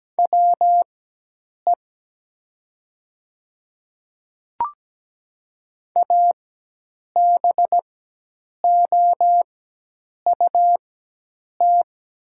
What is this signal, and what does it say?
Morse code, heard as a keyed tone, about 17 wpm, spaced as for 8 wpm, 690 Hz. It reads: WE ABOUT
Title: 4x Character Spacing - Top 100 Words - 17wpm